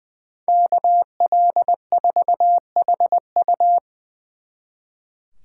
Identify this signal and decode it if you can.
Morse code, heard as a keyed tone, about 21 wpm, 700 Hz. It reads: KL4HU
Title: Speed Racing - USA 2x2 Call Signs 20wpm